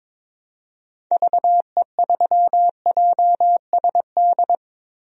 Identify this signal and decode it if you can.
Morse code, heard as a keyed tone, about 22 wpm, 700 Hz. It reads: VE3JSD
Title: DXCC-100 Call Signs - 6 Characters - Repeated 3x - 22wpm